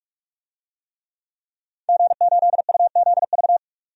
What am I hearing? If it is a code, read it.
Morse code, heard as a keyed tone, about 41 wpm, 700 Hz. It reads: G8UZV
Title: DXCC-100 Call Signs - 5 Characters - 45wpm